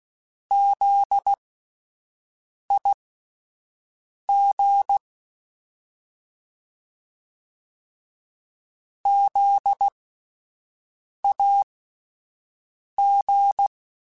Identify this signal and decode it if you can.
Morse code, heard as a keyed tone, about 16 wpm, spaced as for 5 wpm, 800 Hz. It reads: ZIG ZAG
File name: Alphabetical / Alphabetical mp3s / Zs @5